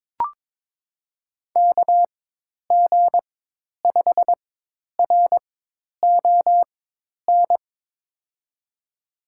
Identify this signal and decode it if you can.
Morse code, heard as a keyed tone, about 22 wpm, spaced as for 10 wpm, 700 Hz. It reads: KG5RON
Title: USA Call Signs - 6 Characters - 4x Character Spacing - 22wpm